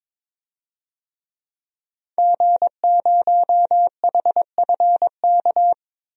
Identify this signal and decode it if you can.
Morse code, heard as a keyed tone, about 21 wpm, 700 Hz. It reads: G0HFK
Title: DXCC-100 Call Signs - 5 Characters - Repeated 3x - 22wpm